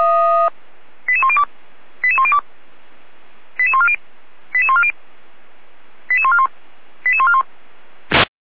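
Five-tone selective call (ZVEI), heard as quick runs of five tones, tones of about 70 ms. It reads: 88102 88102 88140 88140 88141 88141
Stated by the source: ZVEI five-tone selective call received with an SDR.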